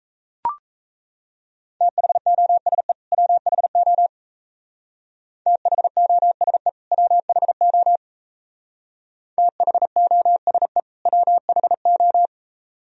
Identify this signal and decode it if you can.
Morse code, heard as a keyed tone, about 34 wpm, 690 Hz. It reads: THOSE WHO THOSE WHO THOSE WHO
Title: Speed-Racing - Sets of 2 Words 28wpm